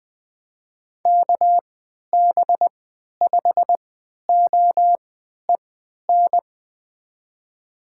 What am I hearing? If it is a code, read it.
Morse code, heard as a keyed tone, about 20 wpm, spaced as for 11 wpm, 700 Hz. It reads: KB5OEN
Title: USA Call Signs - 6 Characters - 3x Character Spacing - 20wpm